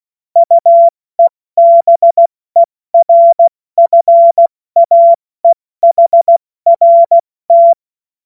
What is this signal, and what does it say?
Morse code, heard as a keyed tone, about 16 wpm, 680 Hz. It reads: UEBERFAEHRT
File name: Morsekurs / Witze / Witz 03 Juni Mai 2025 70 F80